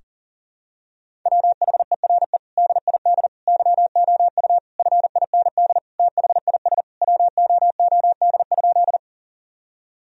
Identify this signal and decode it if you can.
Morse code, heard as a keyed tone, about 40 wpm, 700 Hz. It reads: WHERE DID YOU FIND THIS WOOD?
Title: Sentences from Top 1000 Words - Review Part 2 - 40wpm